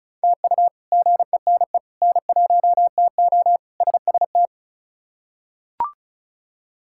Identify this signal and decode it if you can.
Morse code, heard as a keyed tone, about 36 wpm, 700 Hz. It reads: TU GENE N1TO SST
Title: K1USN SST QSO Practice - 35wpm